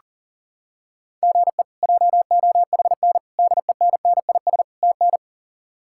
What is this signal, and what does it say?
Morse code, heard as a keyed tone, about 40 wpm, 700 Hz. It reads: GE JOHN DENNIS TN
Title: K1USN SST QSO Practice - 40wpm